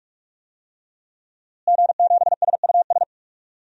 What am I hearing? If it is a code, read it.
Morse code, heard as a keyed tone, about 45 wpm, 700 Hz. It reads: G7SUS
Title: DXCC-100 Call Signs - 5 Characters - Repeated 3x - 45wpm